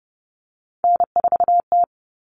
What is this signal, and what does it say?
Morse code, heard as a keyed tone, about 30 wpm, 700 Hz. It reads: N4T
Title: All Cut Numbers - 3-Digit 30wpm